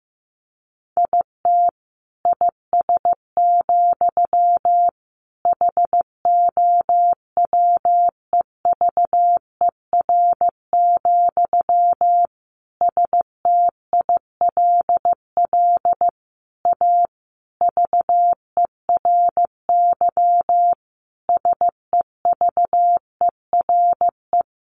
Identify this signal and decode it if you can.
Morse code, heard as a keyed tone, about 15 wpm, 700 Hz. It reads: IT IS, HOWEVER, STILL A VERY SEVERE
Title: Sentences from Top 1400 Words 15wpm